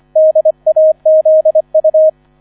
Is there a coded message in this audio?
Yes: Morse code from 0.2 to 2.1 s, heard as a keyed tone, about 25 wpm, 630 Hz, with radio noise. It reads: DAZU